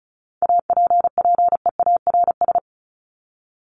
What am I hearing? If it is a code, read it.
Morse code, heard as a keyed tone, about 34 wpm, 700 Hz. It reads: APPEARS
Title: ES:S Suffix - Top 100 Words with Base Words 35wpm